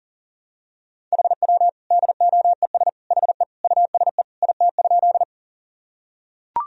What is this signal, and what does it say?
Morse code, heard as a keyed tone, about 40 wpm, 700 Hz. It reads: HW DOES HE USE IT?
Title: Sentences from Top 200 Words and Abbreviations 40wpm